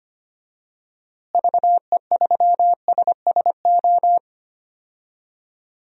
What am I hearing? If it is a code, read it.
Morse code, heard as a keyed tone, about 25 wpm, 700 Hz. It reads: VE3SSO
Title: DXCC-100 Call Signs - 6 Characters - Repeated 3x - 25wpm